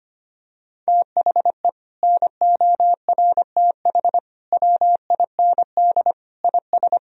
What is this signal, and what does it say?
Morse code, heard as a keyed tone, about 25 wpm, 700 Hz. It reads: THE NORTH WIND IS
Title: Sentences from Top 1000 Words - Review Part 2 - 25wpm